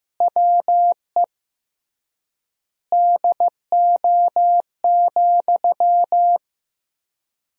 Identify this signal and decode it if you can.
Morse code, heard as a keyed tone, about 14 wpm, 700 Hz. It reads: WE DO,
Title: 3x Word-Spacing - Sentences from Top 100 Words - 15wpm